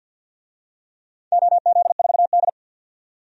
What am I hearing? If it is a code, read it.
Morse code, heard as a keyed tone, about 51 wpm, 700 Hz. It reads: OZ4D
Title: DXCC-100 Call Signs - 4 Characters - 50wpm